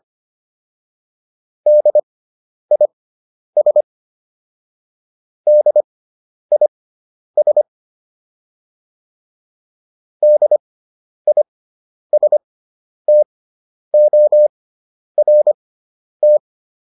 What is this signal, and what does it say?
Morse code, heard as a keyed tone, about 25 wpm, spaced as for 10 wpm, 600 Hz. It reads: DIS DIS DISTORT